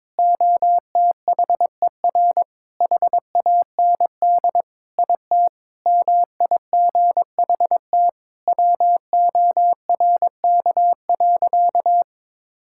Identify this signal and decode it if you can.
Morse code, heard as a keyed tone, about 22 wpm, 700 Hz. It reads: OTHER HAND IT MIGHT WORK.
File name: practice / 22wpm / Sentences with 3-Word Phrases - 22wpm